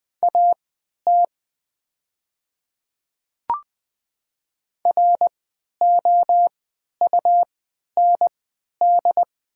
Morse code, heard as a keyed tone, about 20 wpm, spaced as for 11 wpm, 700 Hz. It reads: AT ROUND
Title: 3x Character Spacing - Top 300 Words - 20wpm